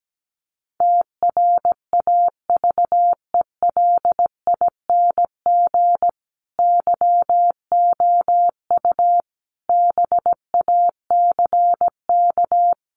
Morse code, heard as a keyed tone, about 17 wpm, 700 Hz. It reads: TRAVELING YOU BACK
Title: A Sound Of Thunder 17wpm